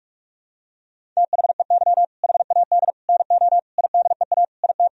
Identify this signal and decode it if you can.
Morse code, heard as a keyed tone, about 46 wpm, 700 Hz. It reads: THEY HAD NO IDEA IT
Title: Sentences from Top 1000 Words - Review Part 1 - 45wpm